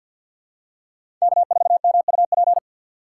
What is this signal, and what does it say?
Morse code, heard as a keyed tone, about 53 wpm, 700 Hz. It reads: K4MUP
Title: USA Call Signs - 5 Characters - 2x Character Spacing - 50wpm